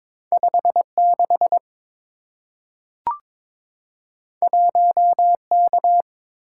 Morse code, heard as a keyed tone, about 22 wpm, 700 Hz. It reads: 56 1K
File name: practice / 22wpm / Two Letter-Number Combinations - 22wpm